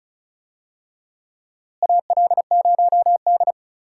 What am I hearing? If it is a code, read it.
Morse code, heard as a keyed tone, about 36 wpm, 700 Hz. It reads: AL0D
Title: USA Call Signs - 4 Characters - Repeated 2x - 35wpm